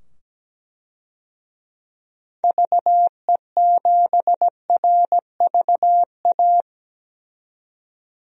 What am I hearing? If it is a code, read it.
Morse code, heard as a keyed tone, about 17 wpm, 700 Hz. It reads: VE7RVA
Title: DXCC-100 Call Signs - 6 Characters - 17wpm